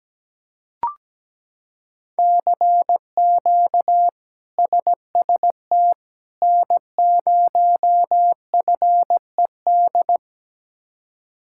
Morse code, heard as a keyed tone, about 17 wpm, 700 Hz. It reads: CQ SST N0FED
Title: K1USN SST QSO Practice - 17wpm